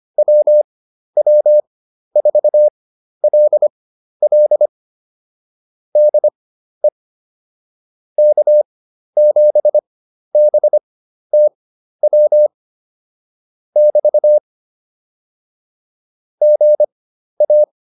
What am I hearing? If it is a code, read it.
Morse code, heard as a keyed tone, about 25 wpm, spaced as for 12 wpm, 600 Hz. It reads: WW4LL DE K7BTW <BT> GA